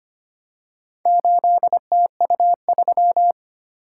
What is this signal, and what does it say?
Morse code, heard as a keyed tone, about 25 wpm, 700 Hz. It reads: 8TU3
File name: practice / 25wpm / Four Letter-Number Combinations - 25wpm